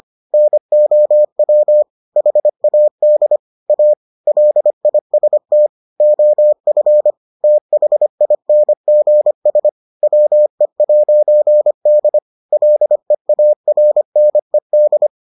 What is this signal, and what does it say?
Morse code, heard as a keyed tone, about 25 wpm, 600 Hz. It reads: NOW HAD A LIST OF THINGS WE'D LEARNED